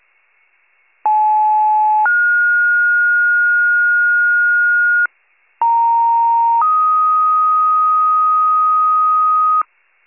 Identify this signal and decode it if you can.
Two-tone page, heard as a one-second tone, then a three-second tone, about 860 then 1430 Hz, about 930 then 1250 Hz, with noise. It is heard from 1.0 to 9.6 s.